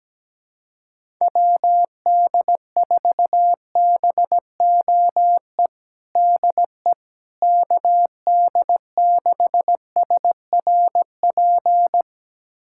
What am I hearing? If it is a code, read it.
Morse code, heard as a keyed tone, about 17 wpm, 700 Hz. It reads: WD4BOE DE KD6SRP
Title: USA Callsign DE Callsign 17wpm